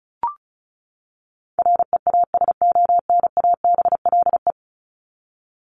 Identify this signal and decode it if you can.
Morse code, heard as a keyed tone, about 34 wpm, 700 Hz. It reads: REASONABLE
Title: Top 1501-1750 Words 35wpm